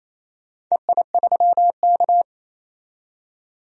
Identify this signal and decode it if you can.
Morse code, heard as a keyed tone, about 28 wpm, 700 Hz. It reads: EI3K
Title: DXCC-100 Call Signs - 4 Characters - 28wpm